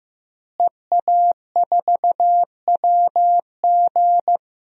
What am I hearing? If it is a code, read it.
Morse code, heard as a keyed tone, about 15 wpm, 700 Hz. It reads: EA4WG
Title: DXCC-100 Call Signs - 5 Characters - Repeated 3x - 15wpm